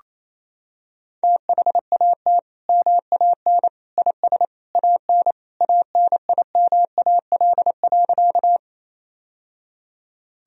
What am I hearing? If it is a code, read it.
Morse code, heard as a keyed tone, about 28 wpm, 700 Hz. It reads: THAT MAN IS AN ANIMAL.
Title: Sentences from Top 1000 Words - Review Part 1 - 28wpm